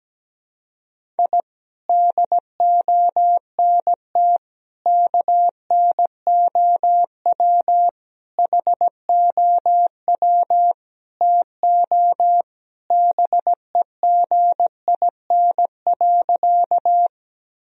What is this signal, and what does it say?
Morse code, heard as a keyed tone, about 17 wpm, 700 Hz. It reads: I DONT KNOW HOW TO BEGIN.
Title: Sentences from Top 1000 Words - Review Part 1 - 17wpm